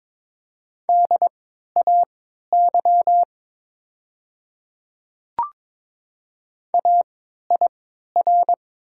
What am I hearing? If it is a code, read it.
Morse code, heard as a keyed tone, about 22 wpm, spaced as for 13 wpm, 700 Hz. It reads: DAY AIR
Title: 3x Character Spacing - Top 200 Words - 22wpm